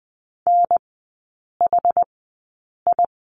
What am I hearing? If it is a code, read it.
Morse code, heard as a keyed tone, about 20 wpm, spaced as for 8 wpm, 700 Hz. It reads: NHI
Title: 100 Lesson 17e - Letter u - Three Characters 20wpm